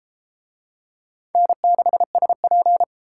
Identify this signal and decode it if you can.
Morse code, heard as a keyed tone, about 30 wpm, 700 Hz. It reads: N6SP